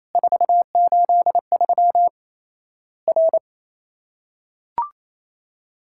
Morse code, heard as a keyed tone, about 28 wpm, 700 Hz. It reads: 483 R
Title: Worked All Europe DX Contest - QTC Exchange - 28wpm